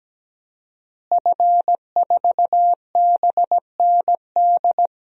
Morse code, heard as a keyed tone, about 17 wpm, 700 Hz. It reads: F4BND
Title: DXCC-100 Call Signs - 5 Characters - Repeated 3x - 17wpm